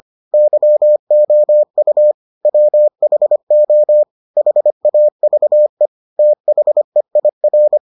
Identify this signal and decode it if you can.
Morse code, heard as a keyed tone, about 25 wpm, 600 Hz. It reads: YOU WHO HAVE THEIR